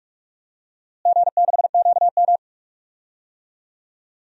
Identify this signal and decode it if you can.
Morse code, heard as a keyed tone, about 42 wpm, 700 Hz. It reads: G6QM
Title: DXCC-100 Call Signs - 4 Characters - Repeated 2x - 45wpm